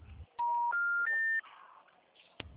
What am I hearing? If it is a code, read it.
Special information tone.